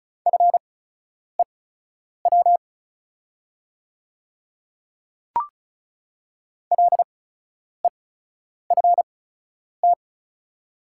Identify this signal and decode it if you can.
Morse code, heard as a keyed tone, about 35 wpm, spaced as for 10 wpm, 690 Hz. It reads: FEW LEFT
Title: 8x Character Spacing - Top 300 Words - 35wpm